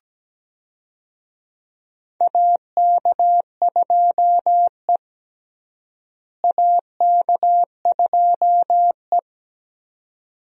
Morse code, heard as a keyed tone, about 17 wpm, 700 Hz. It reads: AK2E AK2E
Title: USA Call Signs - 4 Characters - Repeated 2x - 17wpm